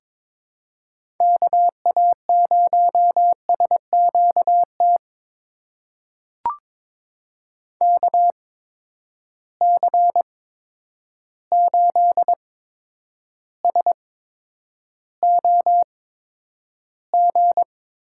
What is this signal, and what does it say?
Morse code, heard as a keyed tone, about 22 wpm, spaced as for 9 wpm, 700 Hz. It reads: KA0SQT KC8SOG
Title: USA Call Signs - 6 Characters - 8x Character Spacing - 22wpm